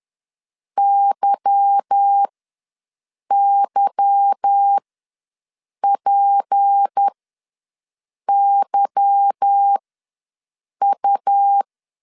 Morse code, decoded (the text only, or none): YYPYU